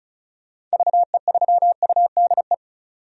Morse code, heard as a keyed tone, about 34 wpm, 700 Hz. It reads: VE3UDE
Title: DXCC-100 Call Signs - 6 Characters - 35wpm